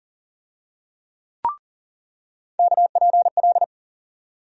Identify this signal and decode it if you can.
Morse code, heard as a keyed tone, about 40 wpm, 700 Hz. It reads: KPL
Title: USA Call Sign Postfixes - 3 Letters 40wpm